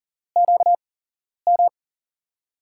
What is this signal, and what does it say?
Morse code, heard as a keyed tone, about 40 wpm, spaced as for 11 wpm, 700 Hz. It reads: QM
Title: USA Call Signs - 5 Characters - 8x Character Spacing - 40wpm